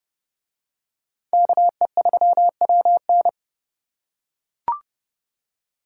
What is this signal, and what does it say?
Morse code, heard as a keyed tone, about 28 wpm, 700 Hz. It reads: KE3WN